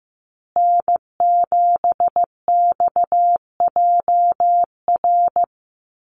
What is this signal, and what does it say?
Morse code, heard as a keyed tone, about 15 wpm, 700 Hz. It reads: N7XJR
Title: USA Call Signs Repeated 3x 15wpm